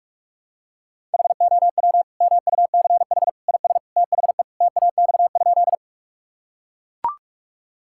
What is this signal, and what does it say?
Morse code, heard as a keyed tone, about 42 wpm, 700 Hz. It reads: HOW MUCH IS THE TAX?
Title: Sentences from Top 1000 Words - Review Part 5 - 45wpm